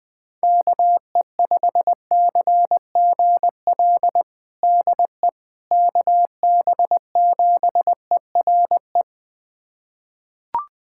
Morse code, heard as a keyed tone, about 20 wpm, 700 Hz. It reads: KE5CGL DE KB7ERE
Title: USA Callsign DE Callsign - 6 Characters - 20wpm